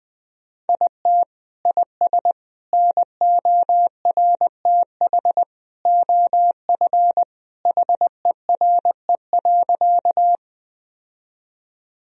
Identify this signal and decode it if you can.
Morse code, heard as a keyed tone, about 20 wpm, 700 Hz. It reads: IT IS NORTH OF HERE.
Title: Sentences from Top 1000 Words - Review Part 1 - 20wpm